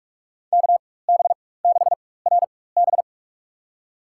Morse code, heard as a keyed tone, about 46 wpm, spaced as for 21 wpm, 700 Hz. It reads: KB6RB
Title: USA Call Signs - 5 Characters - 4x Character Spacing - 45wpm